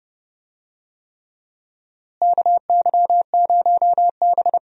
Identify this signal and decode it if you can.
Morse code, heard as a keyed tone, about 30 wpm, 700 Hz. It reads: KY0B
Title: USA Call Signs - 4 Characters - Repeated 3x - 30wpm